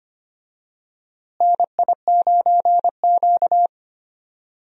Morse code, heard as a keyed tone, about 25 wpm, 700 Hz. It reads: NI9Q
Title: USA Call Signs - 4 Characters - Repeated 2x - 25wpm